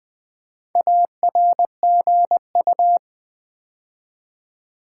Morse code, heard as a keyed tone, about 20 wpm, 700 Hz. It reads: ARGU
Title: Build A Word - 2 Letters At A Time - 8-Letter Words - 20wpm